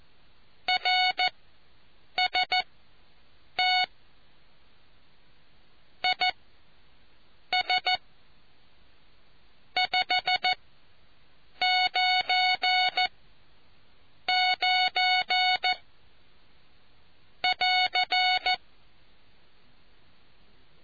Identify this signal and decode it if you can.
Morse code, heard as a keyed tone, about 14 wpm, spaced as for 6 wpm, 740 Hz. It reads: RST IS 599 <AR>